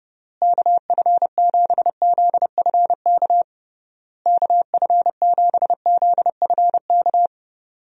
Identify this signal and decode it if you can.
Morse code, heard as a keyed tone, about 31 wpm, 700 Hz. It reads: KF7ZFK KF7ZFK
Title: USA Call Signs - 6 Characters - Repeated 3x - 30wpm